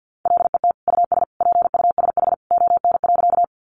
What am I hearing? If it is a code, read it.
Morse code, heard as a keyed tone, about 50 wpm, 700 Hz. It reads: LET US PUSH ON.
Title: Sentences from Top 1300 Words 50wpm